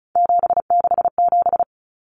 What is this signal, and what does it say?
Morse code, heard as a keyed tone, about 34 wpm, 700 Hz. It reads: 767
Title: Common Cut Numbers - 3-Digit 35wpm